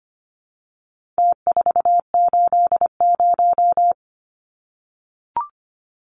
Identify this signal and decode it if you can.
Morse code, heard as a keyed tone, about 25 wpm, 700 Hz. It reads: T480
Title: Numbers - Contest Style - 4-Digit 25wpm